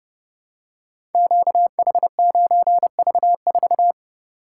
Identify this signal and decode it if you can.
Morse code, heard as a keyed tone, about 32 wpm, 700 Hz. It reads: QH9V4
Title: Code Groups - 5 Letter:Numbers - 30wpm